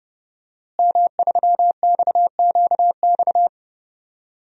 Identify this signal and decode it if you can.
Morse code, heard as a keyed tone, about 30 wpm, 700 Hz. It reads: M3XQX
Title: DXCC-100 Call Signs - 5 Characters - Repeated 3x - 30wpm